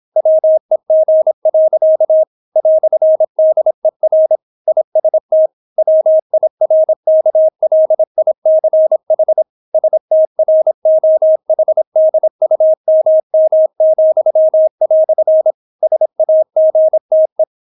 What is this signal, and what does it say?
Morse code, heard as a keyed tone, about 26 wpm, 630 Hz. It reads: WEG. "DER IST WIRKLICH STROHDUMM," SAGTE